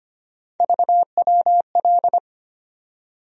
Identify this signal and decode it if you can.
Morse code, heard as a keyed tone, about 25 wpm, 700 Hz. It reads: VWL
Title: USA Call Sign Postfixes - 3 Letters 25wpm